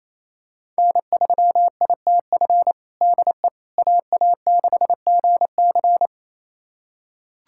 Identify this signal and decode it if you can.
Morse code, heard as a keyed tone, about 28 wpm, 700 Hz. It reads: N3ITF DE AA6GC
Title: USA Callsign DE Callsign - 5 Characters - 28wpm